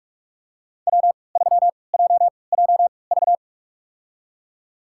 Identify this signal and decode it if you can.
Morse code, heard as a keyed tone, about 47 wpm, spaced as for 26 wpm, 700 Hz. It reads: W3JJV
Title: USA Call Signs - 5 Characters - 3x Character Spacing - 45wpm